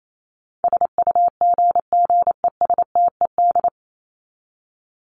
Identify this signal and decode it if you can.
Morse code, heard as a keyed tone, about 28 wpm, 700 Hz. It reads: SUGGESTED
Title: ED Suffix - Top 100 Words with Base Words 28wpm